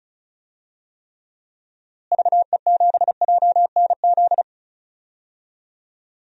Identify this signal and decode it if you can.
Morse code, heard as a keyed tone, about 36 wpm, 700 Hz. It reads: VE7JNZ
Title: DXCC-100 Call Signs - 6 Characters - Repeated 2x - 35wpm